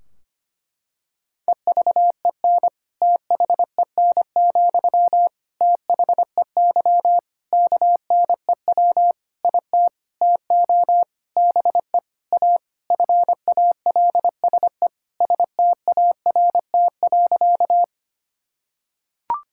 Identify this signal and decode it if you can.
Morse code, heard as a keyed tone, about 25 wpm, 700 Hz. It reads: EVEN THEN, THEY KNEW IT TO BE A FALSE START.
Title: Sentences from Top 1600 Words 25wpm